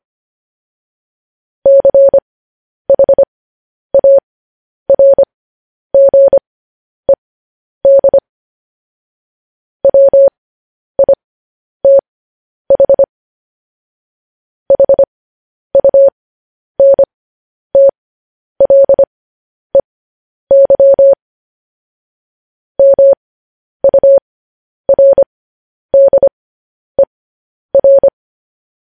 Morse code, heard as a keyed tone, about 25 wpm, spaced as for 10 wpm, 550 Hz. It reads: CHARGED WITH HUNTLEY MURDER